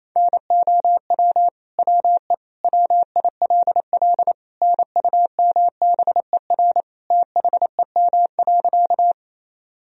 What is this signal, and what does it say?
Morse code, heard as a keyed tone, about 28 wpm, 700 Hz. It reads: NOW WE WILL NUMBER THEM.
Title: Sentences from Top 1000 Words - Review Part 1 - 28wpm